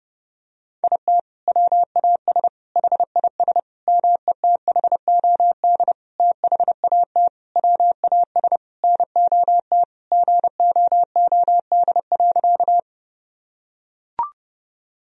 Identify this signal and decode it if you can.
Morse code, heard as a keyed tone, about 32 wpm, 700 Hz. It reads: IT WAS HIS METHOD THAT WAS NOT GOOD.
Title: Sentences from Top 1000 Words - Review Part 3 - 30wpm